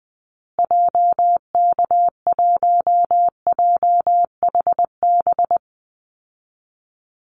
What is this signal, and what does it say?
Morse code, heard as a keyed tone, about 20 wpm, 700 Hz. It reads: JK1JHB
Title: Call Signs Repeated 2x - 20wpm